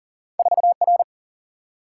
Morse code, heard as a keyed tone, about 39 wpm, 700 Hz. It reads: 4 R
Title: Single-Letter Single-Number 40wpm